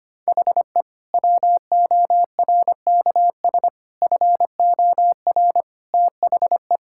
Morse code, heard as a keyed tone, about 25 wpm, 700 Hz. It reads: HE WORKS FOR THE
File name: practice / 25wpm / Sentences with 2-Word Phrases - 25wpm